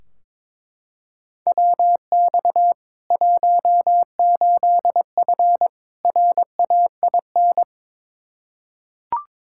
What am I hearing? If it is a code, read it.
Morse code, heard as a keyed tone, about 22 wpm, 700 Hz. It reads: WX 18F RAIN
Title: Practice QSO - Exchange 2 - Info - 22wpm